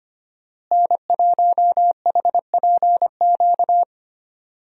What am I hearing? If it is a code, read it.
Morse code, heard as a keyed tone, about 25 wpm, 700 Hz. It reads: N1HPQ
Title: USA Call Signs - 5 Characters - Repeated 2x - 25wpm